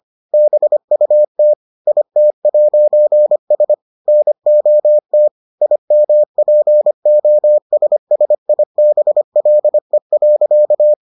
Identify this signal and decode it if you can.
Morse code, heard as a keyed tone, about 25 wpm, 600 Hz. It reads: BUT IT'S NOT IMPOSSIBLE.